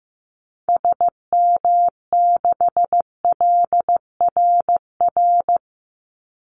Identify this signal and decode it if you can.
Morse code, heard as a keyed tone, about 15 wpm, 700 Hz. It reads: SM6LRR
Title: Call Signs Repeated 3x - 15wpm